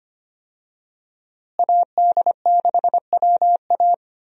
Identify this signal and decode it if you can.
Morse code, heard as a keyed tone, about 25 wpm, 700 Hz. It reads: AD6WA
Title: USA Call Signs - 5 Characters - Repeated 3x - 25wpm